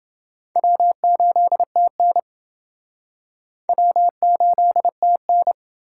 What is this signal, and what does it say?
Morse code, heard as a keyed tone, about 28 wpm, 700 Hz. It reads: W8TN W8TN